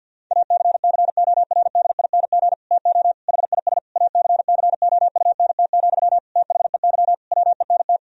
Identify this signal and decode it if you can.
Morse code, heard as a keyed tone, about 48 wpm, 700 Hz. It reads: ACCORDING TO HIS ACCOUNT, THEY WENT
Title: Sentences from Top 1000 Words - Review Part 4 - 50wpm